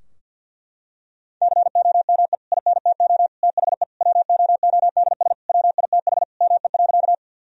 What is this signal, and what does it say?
Morse code, heard as a keyed tone, about 53 wpm, 700 Hz. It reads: COME INTO THE WOODS WITH ME.
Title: Sentences from Top 1000 Words - Review Part 2 - 50wpm